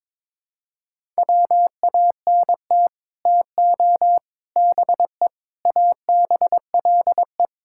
Morse code, heard as a keyed tone, about 22 wpm, 700 Hz. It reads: WANT TO BE ABLE
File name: practice / 22wpm / Build A Phrase - 5-Word Phrases - 22wpm